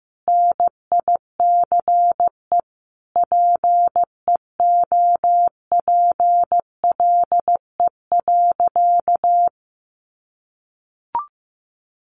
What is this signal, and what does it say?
Morse code, heard as a keyed tone, about 15 wpm, 700 Hz. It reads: NICE PEOPLE.
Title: Sentences from Top 1100 Words 15wpm